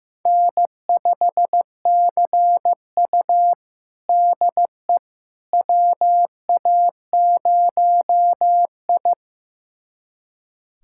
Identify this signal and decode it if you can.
Morse code, heard as a keyed tone, about 15 wpm, 700 Hz. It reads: N5CU DE WA0I
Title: USA Callsign DE Callsign - 4 Characters - 15wpm